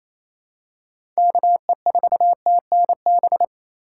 Morse code, heard as a keyed tone, about 28 wpm, 700 Hz. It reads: KE4TNB
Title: USA Call Signs - 2x Character Spacing - 28wpm